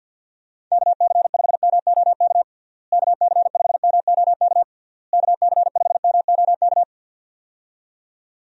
Morse code, heard as a keyed tone, about 52 wpm, 700 Hz. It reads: KC5MOK KC5MOK KC5MOK
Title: USA Call Signs - 6 Characters - Repeated 3x - 50wpm